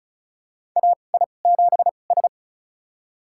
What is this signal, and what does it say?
Morse code, heard as a keyed tone, about 36 wpm, spaced as for 25 wpm, 700 Hz. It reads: AI7S
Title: USA Call Signs - 4 Characters - 2x Character Spacing - 35wpm